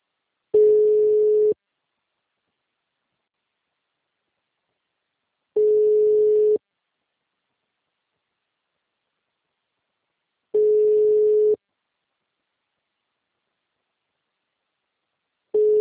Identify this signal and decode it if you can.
Ringback tone (Europe, 425 Hz).